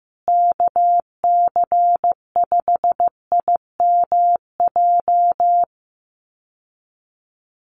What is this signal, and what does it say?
Morse code, heard as a keyed tone, about 15 wpm, 700 Hz. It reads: KC5IMJ
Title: USA Call Signs Repeated 3x 15wpm